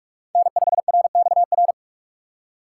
Morse code, heard as a keyed tone, about 46 wpm, 700 Hz. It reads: N5RXR
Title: USA Call Signs - 4x Character Spacing - 45wpm